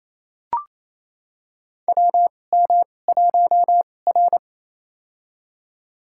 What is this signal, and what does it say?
Morse code, heard as a keyed tone, about 27 wpm, spaced as for 20 wpm, 690 Hz. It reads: WM1R
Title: USA Call Signs - 4 Characters - 2x Character Spacing - 28wpm